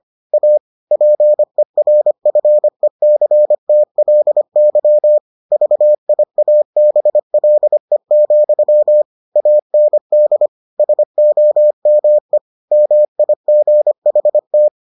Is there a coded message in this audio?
Yes: Morse code from 0.3 to 14.7 s, heard as a keyed tone, about 25 wpm, 600 Hz. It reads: A PERFECTLY VIABLE, AND SOME MIGHT